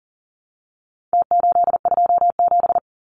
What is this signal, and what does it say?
Morse code, heard as a keyed tone, about 40 wpm, 700 Hz. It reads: T827
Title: Numbers - Contest Style - 4-Digit 40wpm